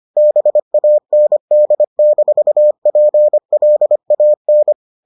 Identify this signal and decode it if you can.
Morse code, heard as a keyed tone, about 25 wpm, 600 Hz. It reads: BAND-PLAN